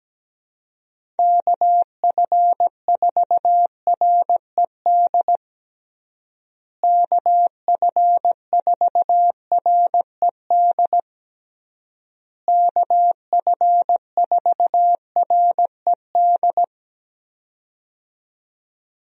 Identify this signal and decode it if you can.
Morse code, heard as a keyed tone, about 17 wpm, 700 Hz. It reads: KF4RED KF4RED KF4RED
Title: USA Call Signs - 6 Characters - Repeated 3x - 17wpm